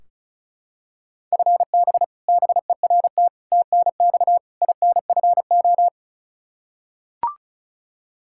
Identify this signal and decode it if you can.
Morse code, heard as a keyed tone, about 34 wpm, 700 Hz. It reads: FB BERT TNX INFO
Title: Practice QSO - Exchange 3 - More Info - 35wpm